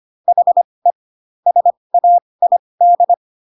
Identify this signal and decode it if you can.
Morse code, heard as a keyed tone, about 25 wpm, spaced as for 20 wpm, 700 Hz. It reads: HE SAID